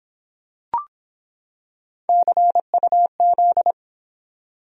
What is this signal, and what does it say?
Morse code, heard as a keyed tone, about 25 wpm, 700 Hz. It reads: CUZ